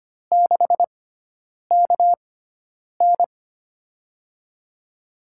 Morse code, heard as a keyed tone, about 25 wpm, spaced as for 9 wpm, 700 Hz. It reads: 6KN
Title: USA Call Signs - 6x Character Spacing - 25wpm